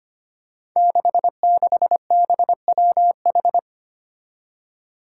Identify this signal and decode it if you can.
Morse code, heard as a keyed tone, about 25 wpm, 700 Hz. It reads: 66BWH
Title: Five Challenging Letter-Number Combinations 25wpm